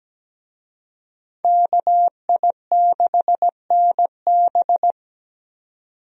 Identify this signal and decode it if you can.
Morse code, heard as a keyed tone, about 17 wpm, 700 Hz. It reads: KI6NB
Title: USA Call Signs - 5 Characters - 17wpm